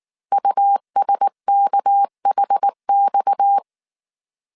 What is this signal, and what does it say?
Morse code, heard as a keyed tone, about 19 wpm, 800 Hz. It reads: USKHX
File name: CW / DET / 1 6 1500C S A 18wpm 2255